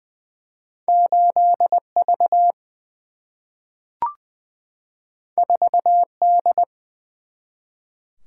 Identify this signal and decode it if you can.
Morse code, heard as a keyed tone, about 20 wpm, 700 Hz. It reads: 8V 4D
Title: Single-Letter Single-Number 20wpm